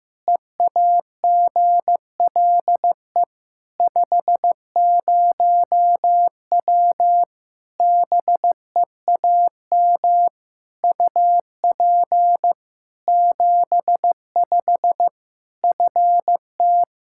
Morse code, heard as a keyed tone, about 15 wpm, 700 Hz. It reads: EAGLE 50W BEAM UP 75 FT
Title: Practice QSO - Exchange 2 - Info - 15wpm